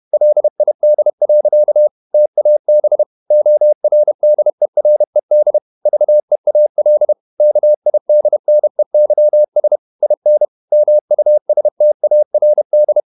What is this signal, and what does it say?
Morse code, heard as a keyed tone, about 31 wpm, 600 Hz. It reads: LID. TAB ORDERED VEAL KIDNEYS IN MUSTARD
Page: Never 31wpm | CW med Gnister